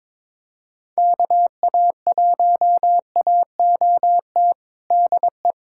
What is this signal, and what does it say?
Morse code, heard as a keyed tone, about 22 wpm, 700 Hz. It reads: KA1AOT DE
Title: USA Callsign DE Callsign 22wpm